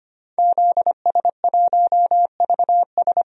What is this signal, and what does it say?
Morse code, heard as a keyed tone, about 25 wpm, 700 Hz. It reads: ZS1VS
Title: DXCC-100 Call Signs - 5 Characters - 25wpm